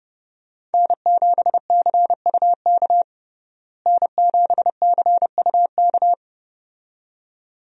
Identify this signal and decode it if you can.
Morse code, heard as a keyed tone, about 32 wpm, 700 Hz. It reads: N7CUK N7CUK
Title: USA Call Signs - 5 Characters - Repeated 2x - 30wpm